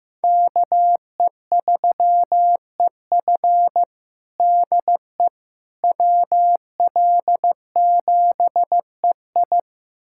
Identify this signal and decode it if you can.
Morse code, heard as a keyed tone, about 15 wpm, 700 Hz. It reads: KE3EF DE WL7EI
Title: USA Callsign DE Callsign - 5 Characters - 15wpm